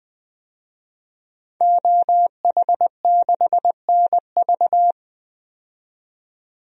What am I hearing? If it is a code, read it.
Morse code, heard as a keyed tone, about 20 wpm, 700 Hz. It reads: OH6NV
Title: DXCC-100 Call Signs - 5 Characters - 20wpm